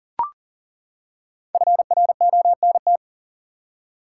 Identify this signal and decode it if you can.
Morse code, heard as a keyed tone, about 40 wpm, 700 Hz. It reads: FRONT
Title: Five Letter Words - 40wpm